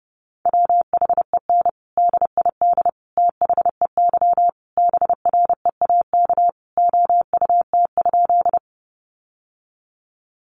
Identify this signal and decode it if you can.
Morse code, heard as a keyed tone, about 30 wpm, 700 Hz. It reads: WHEN DID THEY BREAK OUT?
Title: Sentences from Top 900 Words 30wpm